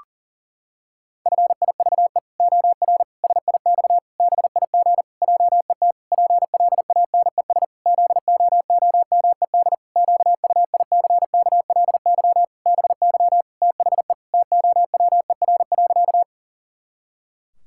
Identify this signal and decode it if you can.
Morse code, heard as a keyed tone, about 37 wpm, 700 Hz. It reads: FIVE OR SIX BIG JET PLANES ZOOMED QUICKLY BY THE TOWER.
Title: Pangram Challenge v2 - 40wpm